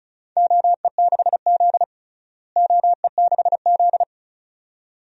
Morse code, heard as a keyed tone, about 36 wpm, 700 Hz. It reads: OE6Z OE6Z
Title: DXCC-100 Call Signs - 4 Characters - Repeated 2x - 35wpm